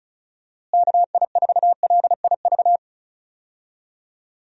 Morse code, heard as a keyed tone, about 34 wpm, 700 Hz. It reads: KI4LIV
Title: USA Call Signs - 6 Characters - Repeated 3x - 35wpm